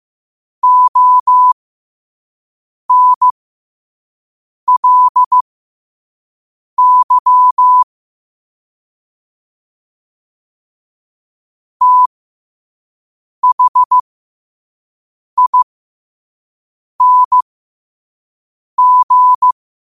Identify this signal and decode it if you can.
Morse code, heard as a keyed tone, about 15 wpm, spaced as for 5 wpm, 1000 Hz. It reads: ONLY THING
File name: qotdfiles / 5wpm